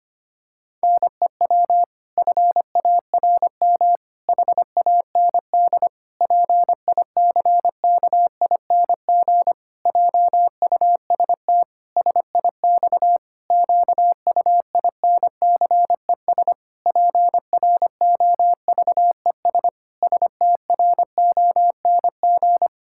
Morse code, heard as a keyed tone, about 25 wpm, 700 Hz. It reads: NEW FARM HAND PICKING JUST SIX QUINCES PROVES STRONG
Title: Pangram Challenge v2 - 25wpm